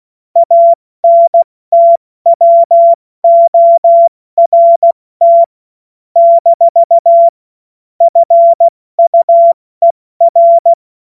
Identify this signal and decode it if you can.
Morse code, heard as a keyed tone, about 16 wpm, 680 Hz. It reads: ANTWORT - FUER
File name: Morsekurs / Witze / Witz 05. April 2022 70 F80